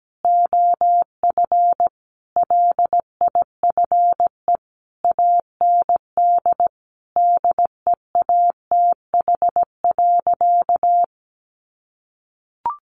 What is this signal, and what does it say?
Morse code, heard as a keyed tone, about 17 wpm, 700 Hz. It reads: OF LIFE AND DEATH.
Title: Sentences from Top 1100 Words 17wpm